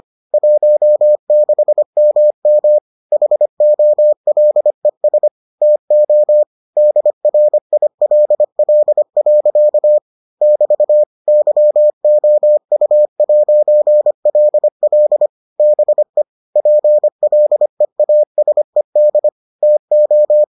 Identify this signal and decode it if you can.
Morse code, heard as a keyed tone, about 25 wpm, 600 Hz. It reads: 16MM HOLES TO DRILL. <BT> YOU'LL BE PLEASED TO